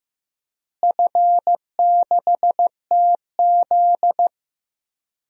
Morse code, heard as a keyed tone, about 15 wpm, 700 Hz. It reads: F6TZ
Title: DXCC-100 Call Signs - 4 Characters - Repeated 3x - 15wpm